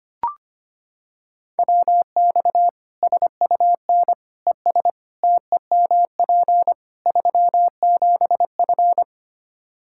Morse code, heard as a keyed tone, about 25 wpm, 700 Hz. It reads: WX SUN ES TEMP 37F
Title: K1USN SST QSO Practice 25wpm